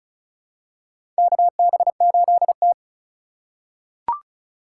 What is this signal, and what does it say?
Morse code, heard as a keyed tone, about 36 wpm, 690 Hz. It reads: KB8T
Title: USA Call Signs - 4 Characters - 2x Character Spacing - 35wpm